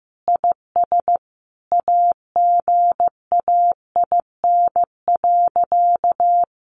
Morse code, heard as a keyed tone, about 14 wpm, 700 Hz. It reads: IS AGAIN.
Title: Sentences from Top 200 Words 15wpm